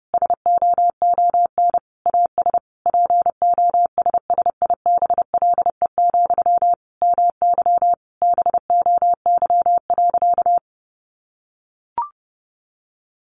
Morse code, heard as a keyed tone, about 30 wpm, 700 Hz. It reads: SOON AS POSSIBLE, MY BOY.
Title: Sentences from Top 800 Words 30wpm